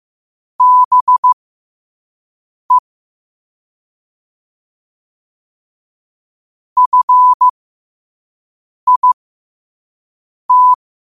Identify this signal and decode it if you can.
Morse code, heard as a keyed tone, about 15 wpm, spaced as for 5 wpm, 1000 Hz. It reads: BE FIT
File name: qotdfiles / 5wpm